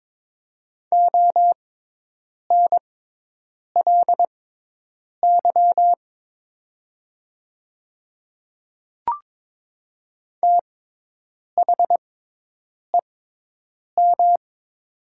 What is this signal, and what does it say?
Morse code, heard as a keyed tone, about 22 wpm, spaced as for 8 wpm, 700 Hz. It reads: ONLY THEM
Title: 6x Character Spacing - Top 200 Words - 22wpm